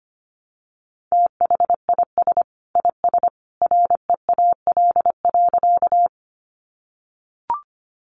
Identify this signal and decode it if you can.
Morse code, heard as a keyed tone, about 25 wpm, 700 Hz. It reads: THIS IS REAL.
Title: Sentences from Top 300 Words 25wpm